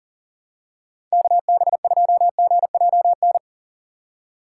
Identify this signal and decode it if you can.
Morse code, heard as a keyed tone, about 37 wpm, 700 Hz. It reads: KB2GJN
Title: USA Call Signs - 6 Characters - 6x Character Spacing - 40wpm